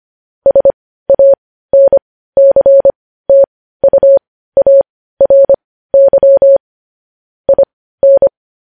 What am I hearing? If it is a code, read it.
Morse code, heard as a keyed tone, about 25 wpm, spaced as for 15 wpm, 550 Hz. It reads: SANCTUARY IN
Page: news-headlines-in-morse-code-at-15-wpm